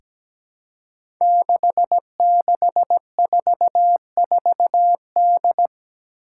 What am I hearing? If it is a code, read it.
Morse code, heard as a keyed tone, about 17 wpm, 700 Hz. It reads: 6644D
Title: Five Challenging Letter-Number Combinations 17wpm